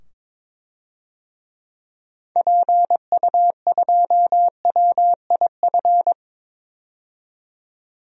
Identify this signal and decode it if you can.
Morse code, heard as a keyed tone, about 22 wpm, 700 Hz. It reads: PU2WIF
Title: DXCC-100 Call Signs - 6 Characters - 22wpm